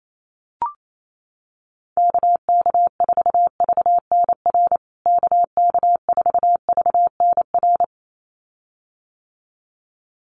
Morse code, heard as a keyed tone, about 28 wpm, 700 Hz. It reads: KK4VNR KK4VNR
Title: USA Call Signs Repeated 2x 28wpm